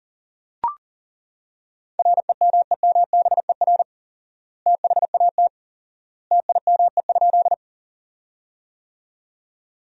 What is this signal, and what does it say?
Morse code, heard as a keyed tone, about 40 wpm, 700 Hz. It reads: REMEMBER THAT TIME?
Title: 4x Word-Spacing - Sentences from Top 400 Words - 40wpm